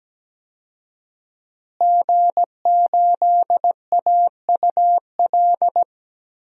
Morse code, heard as a keyed tone, about 17 wpm, 700 Hz. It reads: G8AUL
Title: DXCC-100 Call Signs - 5 Characters - Repeated 2x - 17wpm